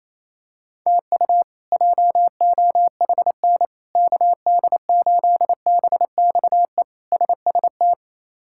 Morse code, heard as a keyed tone, about 28 wpm, 700 Hz. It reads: TU JOHN KD8BXE SST
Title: K1USN SST QSO Practice - 28wpm